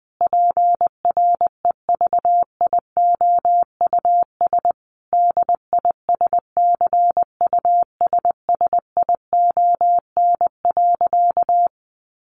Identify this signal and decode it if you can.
Morse code, heard as a keyed tone, about 20 wpm, 700 Hz. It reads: PREVIOUS DISCUSSION.
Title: Sentences from Top 1500 Words 20wpm